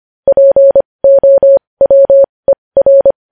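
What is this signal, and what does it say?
Morse code, heard as a keyed tone, about 25 wpm, spaced as for 20 wpm, 550 Hz. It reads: POWER
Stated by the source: Morse practice sending news headlines at 20 wpm.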